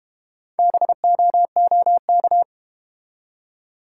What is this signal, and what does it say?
Morse code, heard as a keyed tone, about 32 wpm, 700 Hz. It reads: BOOK